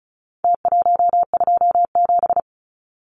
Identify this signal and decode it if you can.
Morse code, heard as a keyed tone, about 34 wpm, 700 Hz. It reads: T127
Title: Numbers - Contest Style - 4-Digit 35wpm